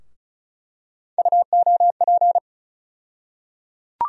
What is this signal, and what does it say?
Morse code, heard as a keyed tone, about 34 wpm, 700 Hz. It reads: UOP
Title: USA Call Sign Postfixes - 3 Letters 35wpm